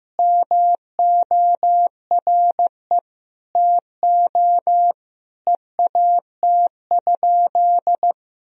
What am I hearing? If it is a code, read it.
Morse code, heard as a keyed tone, about 15 wpm, 700 Hz. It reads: MORE TO EAT?
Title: Sentences from Top 300 Words and Abbreviations 15wpm